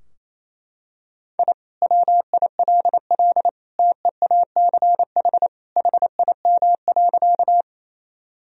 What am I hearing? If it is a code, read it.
Morse code, heard as a keyed tone, about 28 wpm, 700 Hz. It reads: I WILL TEACH HIM.
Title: Sentences from Top 1000 Words - Review Part 2 - 28wpm